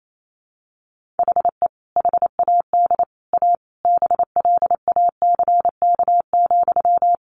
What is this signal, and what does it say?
Morse code, heard as a keyed tone, about 28 wpm, 700 Hz. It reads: HE HAD A BLACK,
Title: Sentences from Top 1500 Words 28wpm